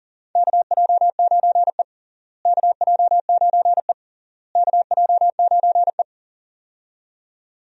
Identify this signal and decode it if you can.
Morse code, heard as a keyed tone, about 40 wpm, 700 Hz. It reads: KJ9E KJ9E KJ9E
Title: USA Call Signs - 4 Characters - Repeated 3x - 40wpm